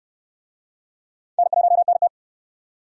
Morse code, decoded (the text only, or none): N1NT